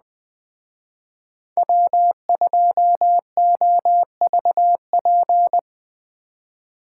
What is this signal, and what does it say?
Morse code, heard as a keyed tone, about 20 wpm, 700 Hz. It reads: W2OVP
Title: Speed Racing - USA 1x3 Call Signs 17wpm